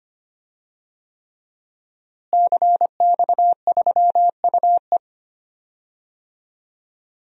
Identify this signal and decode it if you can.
Morse code, heard as a keyed tone, about 25 wpm, 700 Hz. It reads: CX3UE
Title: DXCC-100 Call Signs - 5 Characters - Repeated 3x - 25wpm